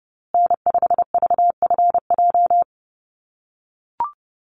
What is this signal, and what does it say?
Morse code, heard as a keyed tone, about 30 wpm, 700 Hz. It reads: N5VFJ
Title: USA Call Signs Repeated 3x 30wpm